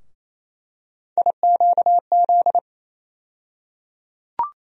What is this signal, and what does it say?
Morse code, heard as a keyed tone, about 28 wpm, 700 Hz. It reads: IQZ
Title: USA Call Sign Postfixes - 3 Letters 28wpm